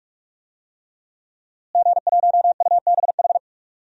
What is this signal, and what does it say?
Morse code, heard as a keyed tone, about 46 wpm, 700 Hz. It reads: G1UBH
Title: DXCC-100 Call Signs - 5 Characters - Repeated 3x - 45wpm